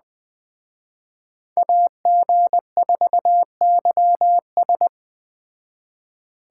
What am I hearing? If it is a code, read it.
Morse code, heard as a keyed tone, about 21 wpm, 700 Hz. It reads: AG4YS
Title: Speed Racing - USA 2x2 Call Signs 17wpm